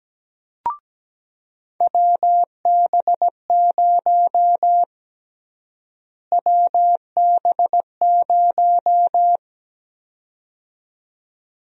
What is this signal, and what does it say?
Morse code, heard as a keyed tone, about 17 wpm, 700 Hz. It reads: WB0 WB0
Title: USA Call Sign Prefixes - Repeated 2x - 17wpm